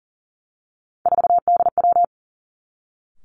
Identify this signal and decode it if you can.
Morse code, heard as a keyed tone, about 39 wpm, 700 Hz. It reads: 4DW
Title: Three Challenging Letter-Number Combinations 40wpm